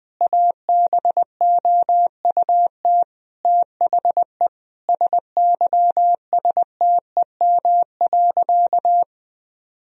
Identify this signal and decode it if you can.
Morse code, heard as a keyed tone, about 20 wpm, 700 Hz. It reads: ABOUT THE SYSTEM.
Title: Sentences from Top 1000 Words - Review Part 3 - 20wpm